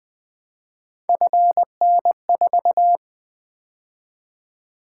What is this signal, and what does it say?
Morse code, heard as a keyed tone, about 20 wpm, 700 Hz. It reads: FN4
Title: Three Letter-Number Combinations - 20wpm